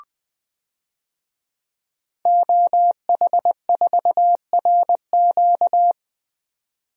Morse code, heard as a keyed tone, about 20 wpm, 700 Hz. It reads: OH4RQ
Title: DXCC-100 Call Signs - 5 Characters - 20wpm